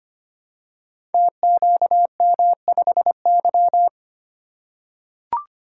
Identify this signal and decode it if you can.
Morse code, heard as a keyed tone, about 25 wpm, 700 Hz. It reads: TQM5Y
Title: Code Groups - 5 Letter:Numbers - 25wpm